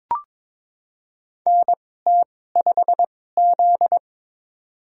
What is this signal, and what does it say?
Morse code, heard as a keyed tone, about 22 wpm, spaced as for 16 wpm, 700 Hz. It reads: NT5Z
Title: USA Call Signs - 2x Character Spacing - 22wpm